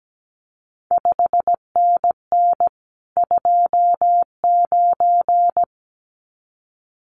Morse code, heard as a keyed tone, about 17 wpm, 700 Hz. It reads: 5NN 29
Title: CQ WW DX Contest Exchange Encore - 17wpm